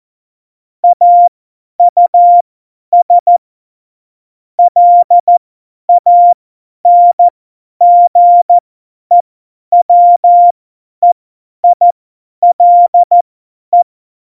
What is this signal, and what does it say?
Morse code, heard as a keyed tone, about 14 wpm, spaced as for 10 wpm, 700 Hz. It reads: AUS LANGEWEILE
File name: Morsekurs / Witze / Witz 05. Januar 2021, 50 F70